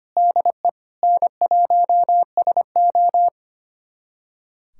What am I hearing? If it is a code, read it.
Morse code, heard as a keyed tone, about 25 wpm, 700 Hz. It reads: DE N1SO
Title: USA Callsign DE Callsign 25wpm